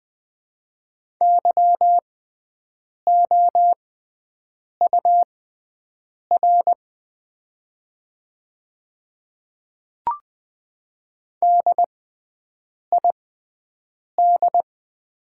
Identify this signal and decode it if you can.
Morse code, heard as a keyed tone, about 20 wpm, spaced as for 7 wpm, 690 Hz. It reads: YOUR DID
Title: 6x Character Spacing - Top 100 Words - 20wpm